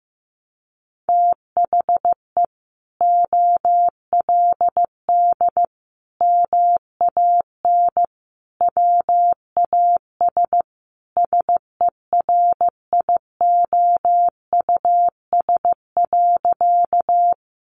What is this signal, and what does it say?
Morse code, heard as a keyed tone, about 15 wpm, 700 Hz. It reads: THE OLD MAN WAS SERIOUS.
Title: Sentences from Top 1100 Words 15wpm